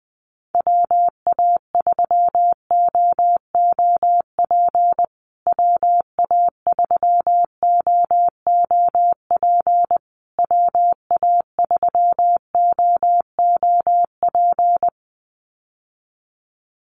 Morse code, heard as a keyed tone, about 20 wpm, 700 Hz. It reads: WA3OOP WA3OOP WA3OOP
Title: USA Call Signs Repeated 3x 20wpm